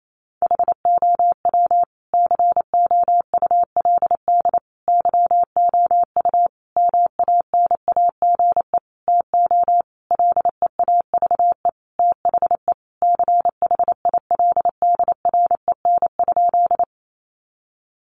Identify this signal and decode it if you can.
Morse code, heard as a keyed tone, about 28 wpm, 700 Hz. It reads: HOW COULD YOU MANAGE TO LEAVE THE CHILDREN?
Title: Sentences from Top 1400 Words 28wpm